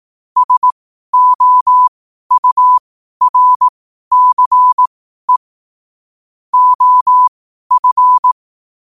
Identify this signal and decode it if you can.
Morse code, heard as a keyed tone, about 18 wpm, spaced as for 12 wpm, 1000 Hz. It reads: SOURCE OF
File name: qotdfiles / 12wpm